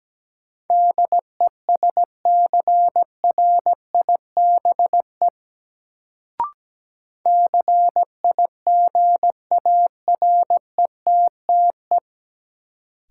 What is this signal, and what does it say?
Morse code, heard as a keyed tone, about 17 wpm, 700 Hz. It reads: DESCRIBE CIGARETTE
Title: Sentences from Top 1000 Words - New Word Drill - 17wpm